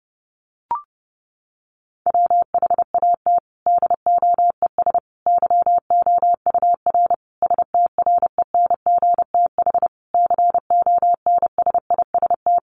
Morse code, heard as a keyed tone, about 30 wpm, 700 Hz. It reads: WHAT DOES YOUR STRENGTH CONSIST
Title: Sentences from Top 1400 Words 30wpm